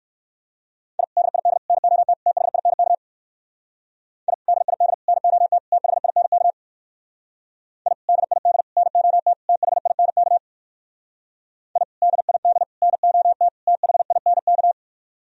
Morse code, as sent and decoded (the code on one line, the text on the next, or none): .. / -.. .. -.. / -. --- - / - .... .. -. -.- / .. / -.. .. -.. / -. --- - / - .... .. -. -.- / .. / -.. .. -.. / -. --- - / - .... .. -. -.- / .. / -.. .. -.. / -. --- - / - .... .. -. -.-
I DID NOT THINK I DID NOT THINK I DID NOT THINK I DID NOT THINK